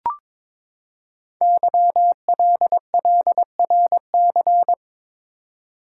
Morse code, heard as a keyed tone, about 22 wpm, 700 Hz. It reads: YLLRC
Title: Code Groups - 5 Letters - 22wpm